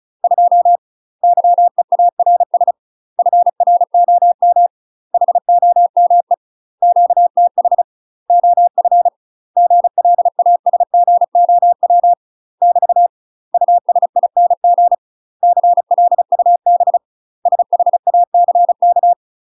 Morse code, heard as a keyed tone, about 35 wpm, 700 Hz. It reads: 2 YEARS FROM HOME QTH OF GLASGOW <BT> USING CLUB SHACK
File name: UK MASTER QSOs TRAINING / 35WPM / UK MASTER QSOs 0007